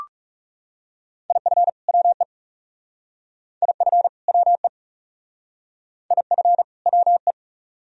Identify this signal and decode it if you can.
Morse code, heard as a keyed tone, about 40 wpm, 700 Hz. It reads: IF WE IF WE IF WE